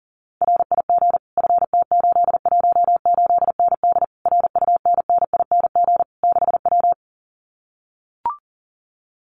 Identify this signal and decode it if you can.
Morse code, heard as a keyed tone, about 40 wpm, 700 Hz. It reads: RIG FT818ND RUNNING 6W
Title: Common QSO Phrases Encore 40wpm